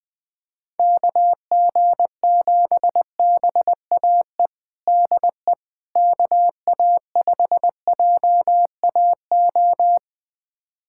Morse code, heard as a keyed tone, about 20 wpm, 700 Hz. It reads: KG7BAE DE KA5JAO
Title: USA Callsign DE Callsign - 6 Characters - 20wpm